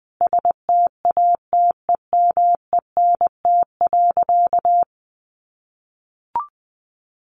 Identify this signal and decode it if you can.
Morse code, heard as a keyed tone, about 20 wpm, 700 Hz. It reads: STATEMENT.
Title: Sentences from Top 1200 Words 20wpm